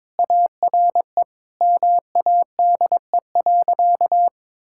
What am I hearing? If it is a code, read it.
Morse code, heard as a keyed tone, about 22 wpm, 700 Hz. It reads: ARE MADE.
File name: practice / 22wpm / Sentences with 4-Word Phrases - 22wpm